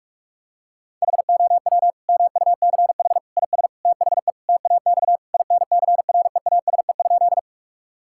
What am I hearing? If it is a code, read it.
Morse code, heard as a keyed tone, about 46 wpm, 700 Hz. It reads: HOW MUCH IS THE TAX INCREASE?
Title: Sentences from Top 1000 Words - Review Part 5 - 45wpm